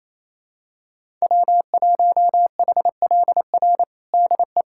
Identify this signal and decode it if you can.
Morse code, heard as a keyed tone, about 28 wpm, 700 Hz. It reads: W1HLR DE
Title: USA Callsign DE Callsign - 5 Characters - 28wpm